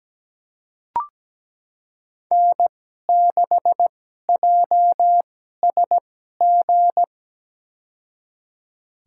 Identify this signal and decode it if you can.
Morse code, heard as a keyed tone, about 17 wpm, spaced as for 12 wpm, 700 Hz. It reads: N6JSG
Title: USA Call Signs - 2x Character Spacing - 17wpm